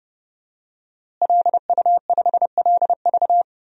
Morse code, heard as a keyed tone, about 32 wpm, 700 Hz. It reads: LU5LV
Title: DXCC-100 Call Signs - 5 Characters - Repeated 3x - 30wpm